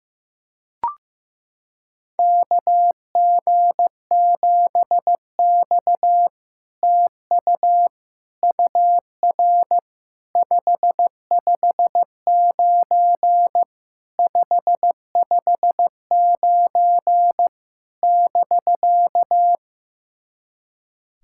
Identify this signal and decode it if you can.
Morse code, heard as a keyed tone, about 15 wpm, 700 Hz. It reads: KG7X TU UR 559 559 <BK>
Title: SOTA QSO Practice 15wpm